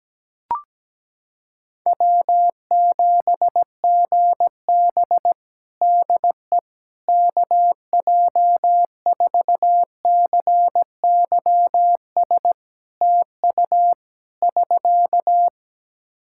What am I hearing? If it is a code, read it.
Morse code, heard as a keyed tone, about 17 wpm, 700 Hz. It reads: W7GB DE KJ4CYS TU <SK>
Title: Practice QSO - Exchange 4 - Ending - 17wpm